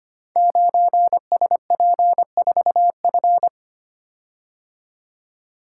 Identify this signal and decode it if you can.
Morse code, heard as a keyed tone, about 26 wpm, 700 Hz. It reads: 9SP4F
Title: Code Groups - 5 Letter:Numbers - 25wpm